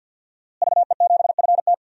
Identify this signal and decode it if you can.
Morse code, heard as a keyed tone, about 51 wpm, 700 Hz. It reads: VE7FT
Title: DXCC-100 Call Signs - 5 Characters - Repeated 2x - 50wpm